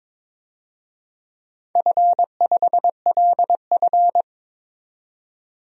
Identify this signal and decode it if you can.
Morse code, heard as a keyed tone, about 22 wpm, 700 Hz. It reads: F5LF
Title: DXCC-100 Call Signs - 4 Characters - 22wpm